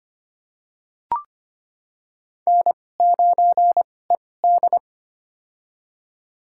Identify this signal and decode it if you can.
Morse code, heard as a keyed tone, about 25 wpm, spaced as for 18 wpm, 700 Hz. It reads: N9ED
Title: USA Call Signs - 4 Characters - 2x Character Spacing - 25wpm